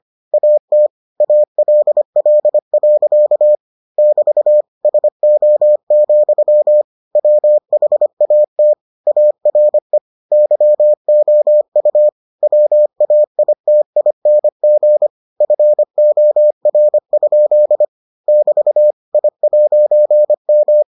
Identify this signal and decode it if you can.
Morse code, heard as a keyed tone, about 25 wpm, 600 Hz. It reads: AT ALL. <BT> SO, WHAT ARE YOU WAITING FOR? <BT> I'M